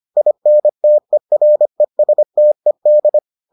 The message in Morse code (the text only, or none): INTERESTED